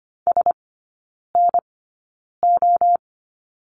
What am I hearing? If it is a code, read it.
Morse code, heard as a keyed tone, about 25 wpm, spaced as for 9 wpm, 700 Hz. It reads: SNO
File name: Lesson 14e - Number 2 - Three Characters 25wpm